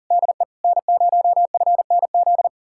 Morse code, heard as a keyed tone, about 51 wpm, 700 Hz. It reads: DE N0FNZ